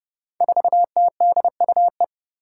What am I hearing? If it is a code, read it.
Morse code, heard as a keyed tone, about 30 wpm, 700 Hz. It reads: 4TDUE
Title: Five Challenging Letter-Number Combinations 30wpm